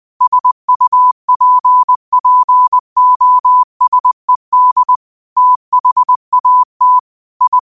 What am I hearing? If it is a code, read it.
Morse code, heard as a keyed tone, about 20 wpm, 1000 Hz. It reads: SUPPOSED THAT I
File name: qotdfiles / 20wpm